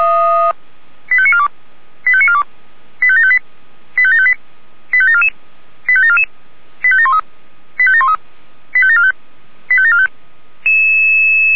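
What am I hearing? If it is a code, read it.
Five-tone selective call (ZVEI), heard as quick runs of five tones, tones of about 70 ms. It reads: 85942 85942 85758 85758 85744 85744 85713 85713 85745 85745
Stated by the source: ZVEI five-tone call 85863 received by SDR.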